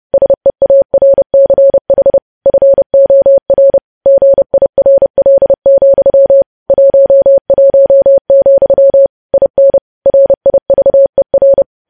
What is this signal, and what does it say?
Morse code, heard as a keyed tone, about 30 wpm, 550 Hz. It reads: SEARCH FOR GIRL, 11, IN RIVER
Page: news headlines in morse code at 30 wpm